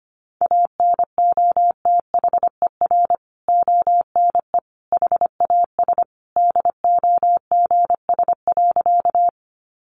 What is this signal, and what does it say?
Morse code, heard as a keyed tone, about 25 wpm, 700 Hz. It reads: ANOTHER ONE HAS DOGS.
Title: Additional Easy Phrases 25wpm